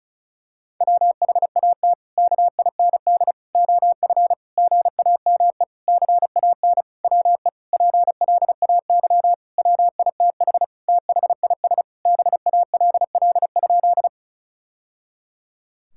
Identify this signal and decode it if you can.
Morse code, heard as a keyed tone, about 36 wpm, 700 Hz. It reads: WHAT KIND OF GAME CAN WE PLAY WITH THIS BALL?
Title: Sentences from Top 1000 Words - Review Part 6 - 35wpm